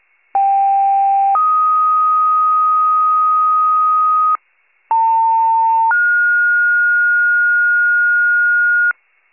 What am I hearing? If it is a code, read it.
Two-tone page, heard as a one-second tone, then a three-second tone, about 790 then 1250 Hz, about 900 then 1430 Hz, with noise. It is heard from 0.3 to 8.9 s.